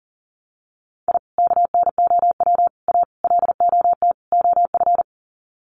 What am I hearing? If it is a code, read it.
Morse code, heard as a keyed tone, about 40 wpm, 700 Hz. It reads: I KNOW A LOT OF
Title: Sets of 5 Words Encore 40wpm